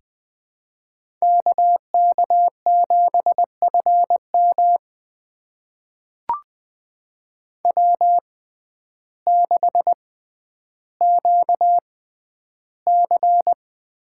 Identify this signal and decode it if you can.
Morse code, heard as a keyed tone, about 21 wpm, spaced as for 11 wpm, 700 Hz. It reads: KK7FM W6QC
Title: USA Call Signs - 5 Characters - 6x Character Spacing - 20wpm